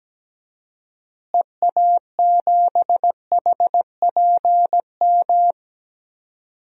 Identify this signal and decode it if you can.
Morse code, heard as a keyed tone, about 17 wpm, 700 Hz. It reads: EA7HPM
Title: DXCC-100 Call Signs - 6 Characters - Repeated 3x - 17wpm